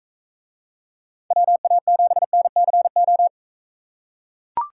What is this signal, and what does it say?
Morse code, heard as a keyed tone, about 44 wpm, 700 Hz. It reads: WA7NCO